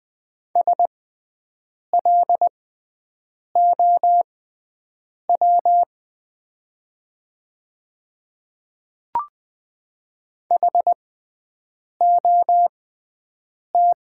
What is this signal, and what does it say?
Morse code, heard as a keyed tone, about 20 wpm, spaced as for 7 wpm, 690 Hz. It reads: SLOW HOT
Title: 6x Character Spacing - Top 500 Words - 20wpm